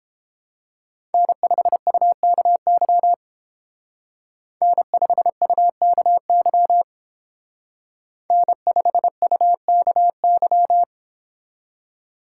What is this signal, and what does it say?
Morse code, heard as a keyed tone, about 28 wpm, 700 Hz. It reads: N5UKY N5UKY N5UKY